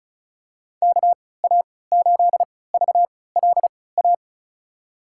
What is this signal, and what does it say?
Morse code, heard as a keyed tone, about 34 wpm, spaced as for 20 wpm, 700 Hz. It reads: KA8VLA
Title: USA Call Signs - 6 Characters - 3x Character Spacing - 35wpm